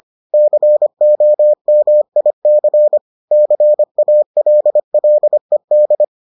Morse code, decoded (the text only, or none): COMIC CALLED